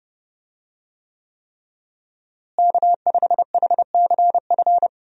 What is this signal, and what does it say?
Morse code, heard as a keyed tone, about 32 wpm, 700 Hz. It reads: K5HCF
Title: USA Call Signs - 5 Characters - Repeated 3x - 30wpm